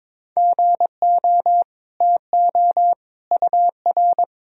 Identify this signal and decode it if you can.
Morse code, heard as a keyed tone, about 22 wpm, 700 Hz. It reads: GO TO UR
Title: Sentences from Top 300 Words and Abbreviations 22wpm